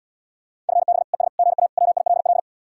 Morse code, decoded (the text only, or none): FB ES GA WILL